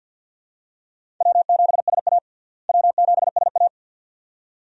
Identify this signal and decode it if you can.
Morse code, heard as a keyed tone, about 46 wpm, 700 Hz. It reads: W7SA W7SA
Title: USA Call Signs - 4 Characters - Repeated 2x - 50wpm